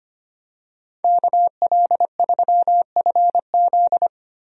Morse code, heard as a keyed tone, about 25 wpm, 700 Hz. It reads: KL3FZ
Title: USA Call Signs - 5 Characters - Repeated 3x - 25wpm